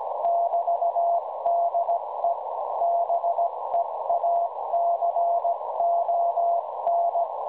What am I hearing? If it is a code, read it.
Morse code, heard as a keyed tone, about 17 wpm, 690 Hz, with noise. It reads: <BT>DE BEACON
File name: WebSDR records / OK0EV websdr recording 1854.0kHz